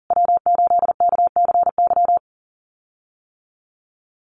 Morse code, heard as a keyed tone, about 40 wpm, 700 Hz. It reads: W8KCY
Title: USA Call Signs Repeated 2x 40wpm